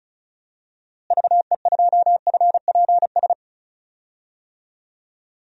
Morse code, heard as a keyed tone, about 36 wpm, 700 Hz. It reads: VE2FPS
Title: DXCC-100 Call Signs - 6 Characters - Repeated 3x - 35wpm